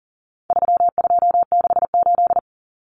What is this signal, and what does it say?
Morse code, heard as a keyed tone, about 40 wpm, 700 Hz. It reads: 3268
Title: Numbers - 4-Digit 40wpm